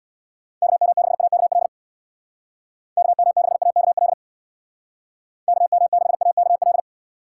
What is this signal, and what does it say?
Morse code, heard as a keyed tone, about 66 wpm, 700 Hz. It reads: KM6AKL KM6AKL KM6AKL